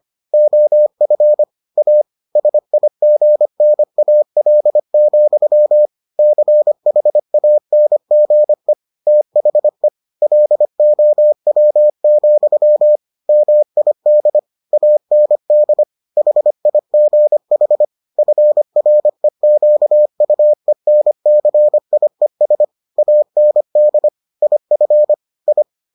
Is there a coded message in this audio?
Yes: Morse code from 0.3 to 25.6 s, heard as a keyed tone, about 25 wpm, 600 Hz. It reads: OF A SIGNAL, CHANGE THE LOW, MID AND HIGH FREQUENCIES AND IF I